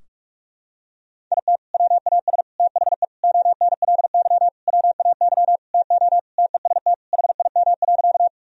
Morse code, heard as a keyed tone, about 46 wpm, 700 Hz. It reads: IT WAS THE ONLY WAY TO TEST HIM.
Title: Sentences from Top 1000 Words - Review Part 3 - 45wpm